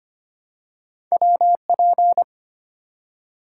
Morse code, heard as a keyed tone, about 26 wpm, 700 Hz. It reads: WP
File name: Two Letter-Number Combinations - 25wpm